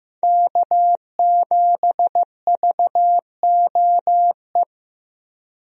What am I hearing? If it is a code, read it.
Morse code, heard as a keyed tone, about 15 wpm, 700 Hz. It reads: K7VOE
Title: USA Call Signs - 5 Characters - Repeated 2x - 15wpm